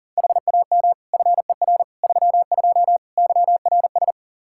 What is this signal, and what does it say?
Morse code, heard as a keyed tone, about 40 wpm, 700 Hz. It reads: HAM FER 32 YRS
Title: Practice QSO - Exchange 3 - More Info - 40wpm